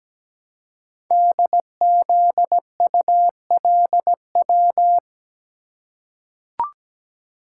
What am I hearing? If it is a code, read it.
Morse code, heard as a keyed tone, about 17 wpm, 700 Hz. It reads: DZULW
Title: Code Groups - 5 Letters - 17wpm